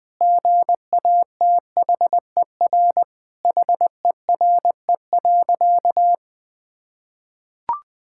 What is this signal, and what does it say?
Morse code, heard as a keyed tone, about 20 wpm, 700 Hz. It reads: GATHER HERE.
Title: Sentences with 6-Word Phrases - 20wpm